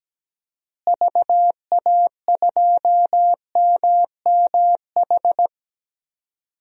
Morse code, heard as a keyed tone, about 17 wpm, 700 Hz. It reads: VA2MMH
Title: DXCC-100 Call Signs - 6 Characters - Repeated 2x - 17wpm